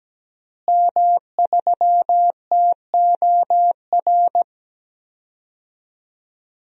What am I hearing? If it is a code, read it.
Morse code, heard as a keyed tone, about 17 wpm, 700 Hz. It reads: M3TOR
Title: DXCC-100 Call Signs - 5 Characters - Repeated 3x - 17wpm